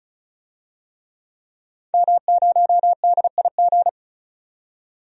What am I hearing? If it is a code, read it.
Morse code, heard as a keyed tone, about 36 wpm, 700 Hz. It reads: M0DIG
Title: DXCC-100 Call Signs - 5 Characters - 35wpm